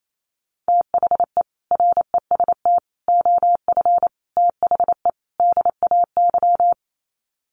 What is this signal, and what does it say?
Morse code, heard as a keyed tone, about 28 wpm, 700 Hz. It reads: THE REST OF THE DAY
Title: Sets of 5 Words 28wpm